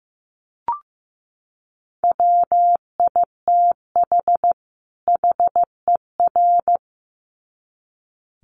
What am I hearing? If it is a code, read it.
Morse code, heard as a keyed tone, about 15 wpm, 700 Hz. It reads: WITH HER
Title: Sets of 2 Words 15wpm